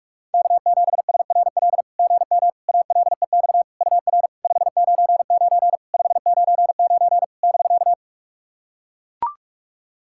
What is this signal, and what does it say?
Morse code, heard as a keyed tone, about 41 wpm, 690 Hz. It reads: K7SRL GM ALEX UR 599 599 <BK>
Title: SOTA QSO Practice 45wpm